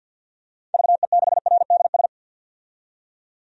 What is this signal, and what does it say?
Morse code, heard as a keyed tone, about 50 wpm, 700 Hz. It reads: VE6RDS
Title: DXCC-100 Call Signs - 6 Characters - 50wpm